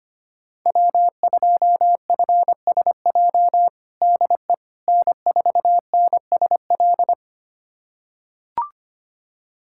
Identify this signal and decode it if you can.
Morse code, heard as a keyed tone, about 25 wpm, 700 Hz. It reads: W2FSJ DE N4NSL
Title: USA Callsign DE Callsign - 5 Characters - 25wpm